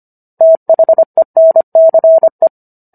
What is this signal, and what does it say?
Morse code, heard as a keyed tone, about 25 wpm, 650 Hz. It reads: THENCE